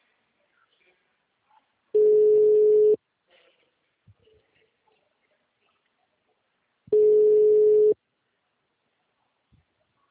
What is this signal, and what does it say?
Ringback tone (Europe, 425 Hz).